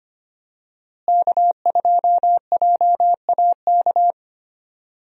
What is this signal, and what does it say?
Morse code, heard as a keyed tone, about 25 wpm, 700 Hz. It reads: K2JAK
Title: USA Call Signs - 5 Characters - Repeated 2x - 25wpm